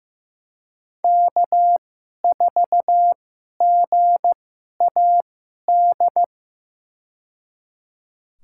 Morse code, heard as a keyed tone, about 15 wpm, spaced as for 11 wpm, 700 Hz. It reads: K4GAD
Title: USA Call Signs - 2x Character Spacing - 15wpm